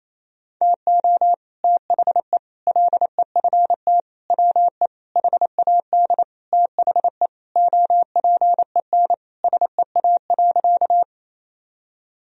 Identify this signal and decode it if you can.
Morse code, heard as a keyed tone, about 28 wpm, 700 Hz. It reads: TO THE LEFT WE HAD THE OPEN SEA.
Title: Sentences from Top 1000 Words - Review Part 1 - 28wpm